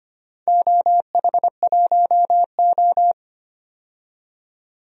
Morse code, heard as a keyed tone, about 25 wpm, 700 Hz. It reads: OH1O
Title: DXCC-100 Call Signs - 4 Characters - 25wpm